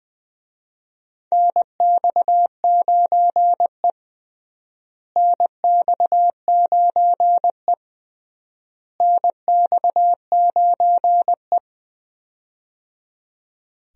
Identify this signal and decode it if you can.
Morse code, heard as a keyed tone, about 20 wpm, 700 Hz. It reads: NX9E NX9E NX9E
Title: USA Call Signs - 4 Characters - Repeated 3x - 20wpm